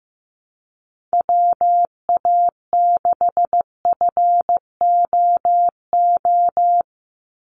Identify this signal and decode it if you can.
Morse code, heard as a keyed tone, about 15 wpm, 700 Hz. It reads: WA6FOO
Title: USA Call Signs 15wpm